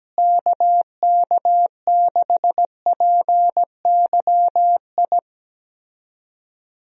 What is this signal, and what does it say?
Morse code, heard as a keyed tone, about 17 wpm, 700 Hz. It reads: KK6PYI
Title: USA Call Signs - 6 Characters - Repeated 3x - 17wpm